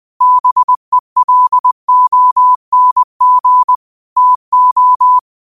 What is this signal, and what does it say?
Morse code, heard as a keyed tone, about 20 wpm, 1000 Hz. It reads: BELONG TO
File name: qotdfiles / 20wpm